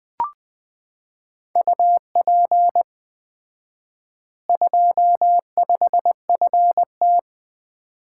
Morse code, heard as a keyed tone, about 20 wpm, 700 Hz. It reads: UP 25FT
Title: Common QSO Phrases - 4x Word-Spacing - 20wpm